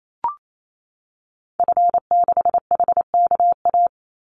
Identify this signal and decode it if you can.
Morse code, heard as a keyed tone, about 28 wpm, 700 Hz. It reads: F6HKA
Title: Call Signs Repeated 2x - 28wpm